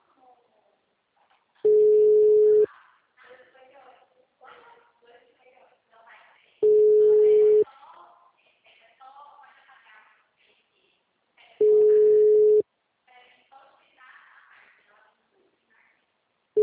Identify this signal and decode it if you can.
Ringback tone (Europe, 425 Hz), with speech.
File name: out- -2007- - - .9051